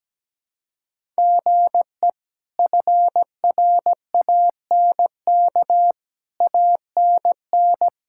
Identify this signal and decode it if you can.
Morse code, heard as a keyed tone, about 17 wpm, 700 Hz. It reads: GE FRANK ANN
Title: K1USN SST QSO Practice - 17wpm